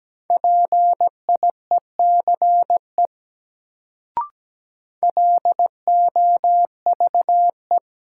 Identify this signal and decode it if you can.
Morse code, heard as a keyed tone, about 17 wpm, 700 Hz. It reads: PIECE LOVE
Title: Sentences from Top 400 Words - New Word Drill - 17wpm